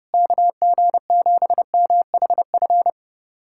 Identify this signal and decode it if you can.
Morse code, heard as a keyed tone, about 29 wpm, 700 Hz. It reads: KG7MHF
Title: USA Callsign DE Callsign - 6 Characters - 30wpm